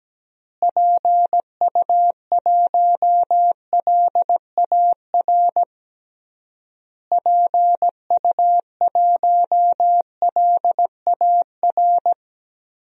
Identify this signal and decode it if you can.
Morse code, heard as a keyed tone, about 17 wpm, 700 Hz. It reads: PU1LAR PU1LAR
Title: DXCC-100 Call Signs - 6 Characters - Repeated 3x - 17wpm